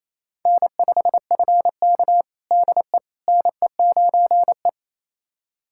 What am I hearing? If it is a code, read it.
Morse code, heard as a keyed tone, about 27 wpm, 700 Hz. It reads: N5FK DE NE9E
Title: USA Callsign DE Callsign - 4 Characters - 28wpm